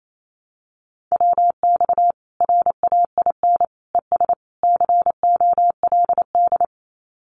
Morse code, heard as a keyed tone, about 28 wpm, 700 Hz. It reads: WX RAIN ES COLD
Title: Common QSO Phrases 28wpm